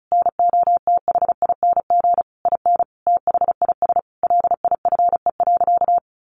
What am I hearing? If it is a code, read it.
Morse code, heard as a keyed tone, about 34 wpm, 700 Hz. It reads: NOTHING IN THIS LIFE.
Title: Sentences from Top 1200 Words 35wpm